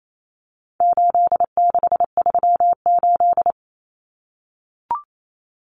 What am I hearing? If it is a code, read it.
Morse code, heard as a keyed tone, about 28 wpm, 700 Hz. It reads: 8638
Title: Common Cut Numbers - 4-Digit 28wpm